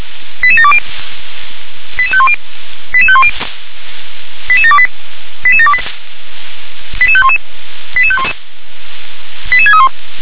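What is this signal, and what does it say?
Five-tone selective call (ZVEI), heard as quick runs of five tones, tones of about 70 ms. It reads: 88720 88510 88510 88728 88728 88511 88511 88631
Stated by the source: ZVEI five-tone call received by SDR.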